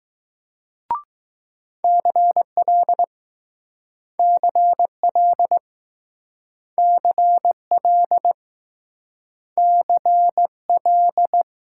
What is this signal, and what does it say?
Morse code, heard as a keyed tone, about 20 wpm, 700 Hz. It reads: CL CL CL CL